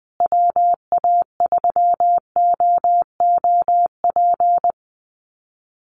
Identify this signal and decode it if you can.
Morse code, heard as a keyed tone, about 20 wpm, 700 Hz. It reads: WA3OOP
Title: USA Call Signs Repeated 3x 20wpm